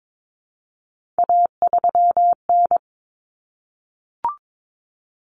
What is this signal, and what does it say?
Morse code, heard as a keyed tone, about 22 wpm, 700 Hz. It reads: A3N
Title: Common Cut Numbers - 3-Digit 22wpm